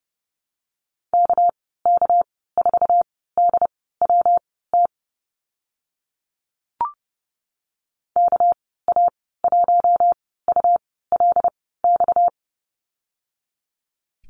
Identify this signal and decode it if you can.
Morse code, heard as a keyed tone, about 30 wpm, spaced as for 17 wpm, 700 Hz. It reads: KK4DWT KA1ULX
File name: Lesson 37g - Letter x - Call Signs 30wpm